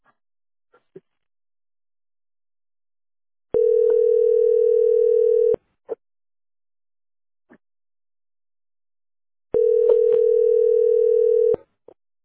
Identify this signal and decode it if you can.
Ringback tone (North America), 440 + 480 Hz.